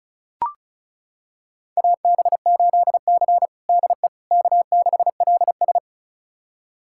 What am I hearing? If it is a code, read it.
Morse code, heard as a keyed tone, about 36 wpm, 700 Hz. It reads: AB8C DE K6LS
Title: USA Callsign DE Callsign - 4 Characters - 35wpm